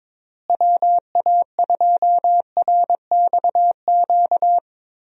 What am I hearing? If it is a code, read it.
Morse code, heard as a keyed tone, about 22 wpm, 700 Hz. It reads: WA2RXQ
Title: USA Call Signs - 6 Characters - Repeated 2x - 22wpm